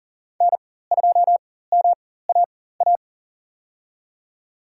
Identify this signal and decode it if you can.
Morse code, heard as a keyed tone, about 40 wpm, spaced as for 19 wpm, 700 Hz. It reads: N2MAA
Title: USA Call Signs - 4x Character Spacing - 40wpm